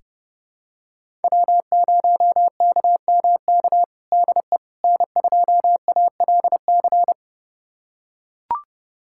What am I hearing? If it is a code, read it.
Morse code, heard as a keyed tone, about 30 wpm, 700 Hz. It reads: W0KMK DE N2ALC
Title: USA Callsign DE Callsign - 5 Characters - 30wpm